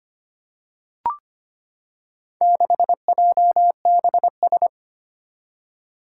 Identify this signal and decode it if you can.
Morse code, heard as a keyed tone, about 25 wpm, 700 Hz. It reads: 6JBS
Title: Four Challenging Letter-Number Combinations 25wpm